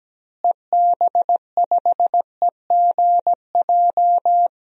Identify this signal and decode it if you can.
Morse code, heard as a keyed tone, about 17 wpm, 700 Hz. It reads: EB5EGJ
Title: DXCC-100 Call Signs - 6 Characters - Repeated 2x - 17wpm